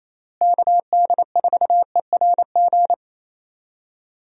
Morse code, heard as a keyed tone, about 27 wpm, 700 Hz. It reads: KD4ERG
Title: USA Call Signs - 6 Characters - Repeated 2x - 28wpm